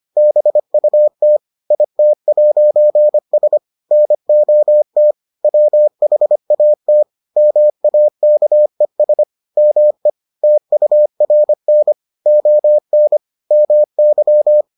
Morse code, heard as a keyed tone, about 25 wpm, 600 Hz. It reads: BUT IT'S NOT WHAT MAKES ME TURN ON MY